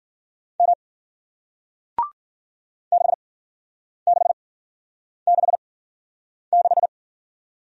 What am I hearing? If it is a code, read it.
Morse code, heard as a keyed tone, about 56 wpm, spaced as for 9 wpm, 700 Hz. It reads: M 6666